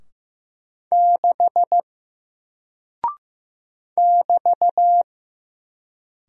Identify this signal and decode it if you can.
Morse code, heard as a keyed tone, about 15 wpm, 700 Hz. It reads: 6 <BT>
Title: Single Character (Letter:Number:Punctuation:Prosign) Alt 15wpm